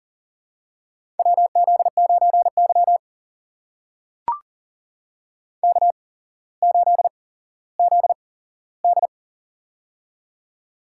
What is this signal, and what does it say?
Morse code, heard as a keyed tone, about 40 wpm, spaced as for 17 wpm, 700 Hz. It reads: WZ9Y K8ZD
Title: USA Call Signs - 4 Characters - 8x Character Spacing - 40wpm